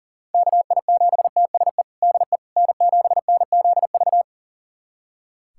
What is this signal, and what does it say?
Morse code, heard as a keyed tone, about 37 wpm, 700 Hz. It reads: KI7TSE DE N7NZV
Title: USA Callsign DE Callsign 40wpm